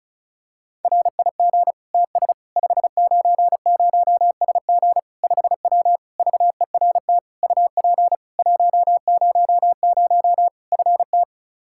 Morse code, heard as a keyed tone, about 36 wpm, 700 Hz. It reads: RIG TS 590SG 5W VERT UP 100 FT
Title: Practice QSO - Exchange 2 - Info - 35wpm